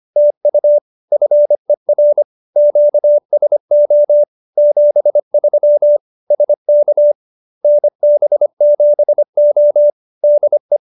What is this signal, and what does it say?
Morse code, heard as a keyed tone, about 25 wpm, 590 Hz. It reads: TU FER QSO 73 SK NB7O DE